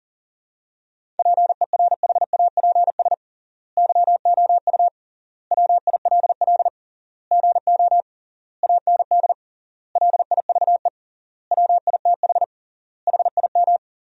Morse code, heard as a keyed tone, about 37 wpm, 700 Hz. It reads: PERHAPS YOU WILL GO AND LIVE WITH HIM
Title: 3x Word-Spacing - Sentences from Top 500 Words - 40wpm